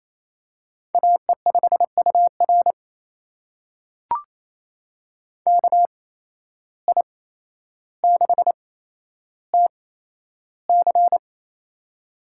Morse code, heard as a keyed tone, about 27 wpm, spaced as for 12 wpm, 700 Hz. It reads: AE5UR KI6TC
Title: USA Call Signs - 5 Characters - 8x Character Spacing - 28wpm